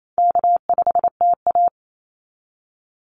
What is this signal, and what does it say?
Morse code, heard as a keyed tone, about 28 wpm, 700 Hz. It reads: K5TA
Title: Easy Call Signs - 28wpm